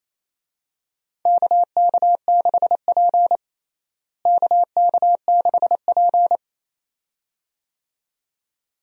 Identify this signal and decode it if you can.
Morse code, heard as a keyed tone, about 28 wpm, 700 Hz. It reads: KK6P KK6P
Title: USA Call Signs - 4 Characters - Repeated 2x - 28wpm